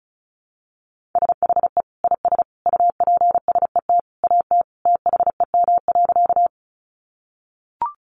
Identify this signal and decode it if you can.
Morse code, heard as a keyed tone, about 34 wpm, 700 Hz. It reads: SHE IS UPSET AT THEM.
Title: Sentences from Top 800 Words 35wpm